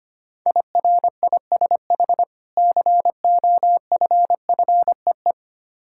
Morse code, heard as a keyed tone, about 25 wpm, 700 Hz. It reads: IRISH COFFEE
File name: practice / 25wpm / Difficult Words with Hints - 25wpm